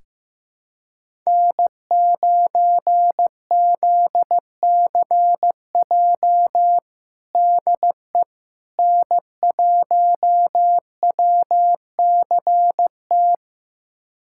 Morse code, heard as a keyed tone, about 15 wpm, 700 Hz. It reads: N9ZCJ DE N1WCT
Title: USA Callsign DE Callsign - 5 Characters - 15wpm